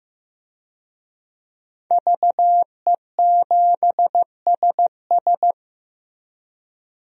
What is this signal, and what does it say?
Morse code, heard as a keyed tone, about 15 wpm, 700 Hz. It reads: VE7SS
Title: DXCC-100 Call Signs - 5 Characters - Repeated 3x - 15wpm